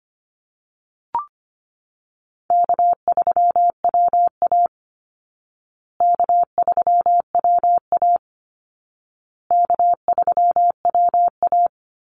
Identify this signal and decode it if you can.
Morse code, heard as a keyed tone, about 25 wpm, 700 Hz. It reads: K3WA K3WA K3WA
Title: Easy Call Signs Repeated 3x - 25wpm